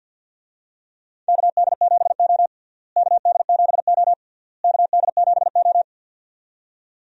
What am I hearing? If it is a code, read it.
Morse code, heard as a keyed tone, about 53 wpm, 700 Hz. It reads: KD7O KD7O KD7O
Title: USA Call Signs - 4 Characters - Repeated 3x - 50wpm